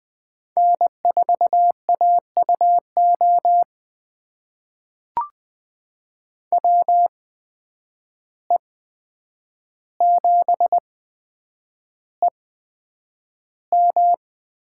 Morse code, heard as a keyed tone, about 20 wpm, spaced as for 9 wpm, 700 Hz. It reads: N4AUO WE7EM
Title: USA Call Signs - 5 Characters - 8x Character Spacing - 20wpm